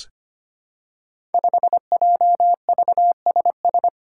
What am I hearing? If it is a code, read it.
Morse code, heard as a keyed tone, about 25 wpm, 700 Hz. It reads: 5JVSS
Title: Five Challenging Letter-Number Combinations 25wpm